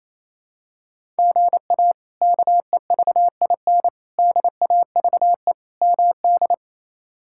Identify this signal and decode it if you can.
Morse code, heard as a keyed tone, about 28 wpm, 700 Hz. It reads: GA KEVIN DAVE MD
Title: K1USN SST QSO Practice - 28wpm